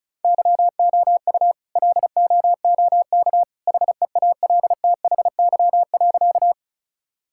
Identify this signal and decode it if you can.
Morse code, heard as a keyed tone, about 36 wpm, 700 Hz. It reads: YOU LOOK HEALTHY.
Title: Sentences from Top 1000 Words - Review Part 5 - 35wpm